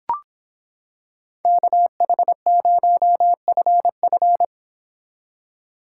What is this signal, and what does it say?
Morse code, heard as a keyed tone, about 26 wpm, 700 Hz. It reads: KH0FF